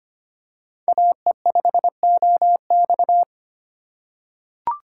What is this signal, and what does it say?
Morse code, heard as a keyed tone, about 25 wpm, 700 Hz. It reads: AE5OX
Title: USA Call Signs - 5 Characters - 3x Character Spacing - 25wpm